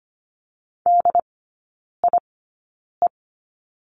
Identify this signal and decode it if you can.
Morse code, heard as a keyed tone, about 25 wpm, spaced as for 9 wpm, 700 Hz. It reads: DIE
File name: course / 25:10wpm / 093 Lesson 16e - Letter c - Three Characters 25wpm